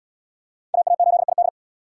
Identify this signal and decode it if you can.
Morse code, heard as a keyed tone, about 77 wpm, 700 Hz. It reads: AA8ID